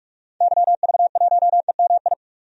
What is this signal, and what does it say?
Morse code, heard as a keyed tone, about 46 wpm, 700 Hz. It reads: YV1EMI
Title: DXCC-100 Call Signs - 6 Characters - 45wpm